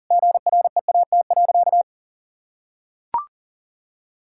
Morse code, heard as a keyed tone, about 40 wpm, 700 Hz. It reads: GREAT.
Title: Sentences from Top 1000 Words - Review Part 5 - 40wpm